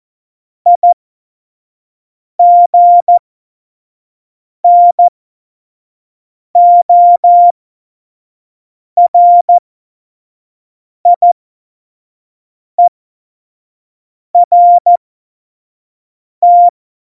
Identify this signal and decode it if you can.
Morse code, heard as a keyed tone, about 14 wpm, spaced as for 5 wpm, 700 Hz. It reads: IGNORIERT
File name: Morsekurs / Witze / Witz 05. Dezember 2017, 25 F70